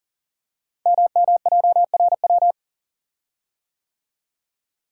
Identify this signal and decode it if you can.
Morse code, heard as a keyed tone, about 42 wpm, 700 Hz. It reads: MMJRW
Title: Code Groups - 5 Letters - 40wpm